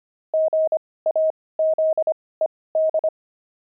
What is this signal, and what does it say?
Morse code, heard as a keyed tone, about 25 wpm, spaced as for 18 wpm, 630 Hz. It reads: GAZED
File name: pangram-fs-18-cs-25-squirrel